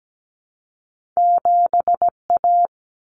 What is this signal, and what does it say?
Morse code, heard as a keyed tone, about 17 wpm, 700 Hz. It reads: 7A
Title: Common Cut Numbers - 2-Digit 17wpm